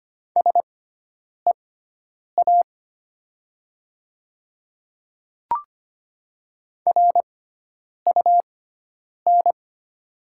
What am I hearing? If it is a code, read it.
Morse code, heard as a keyed tone, about 25 wpm, spaced as for 9 wpm, 700 Hz. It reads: SEA RUN
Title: 6x Character Spacing - Top 300 Words - 25wpm